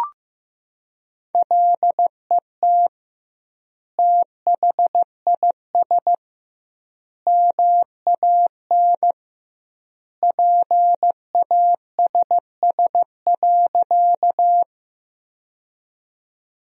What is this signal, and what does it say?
Morse code, heard as a keyed tone, about 15 wpm, 700 Hz. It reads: LET THIS MAN PASS.
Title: 2x Word-Spacing - Sentences from Top 400 Words - 15wpm